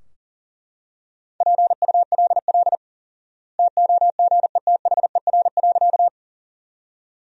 Morse code, heard as a keyed tone, about 37 wpm, 700 Hz. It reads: PULL TOGETHER.
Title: 4x Word-Spacing - Sentences from Top 400 Words - 40wpm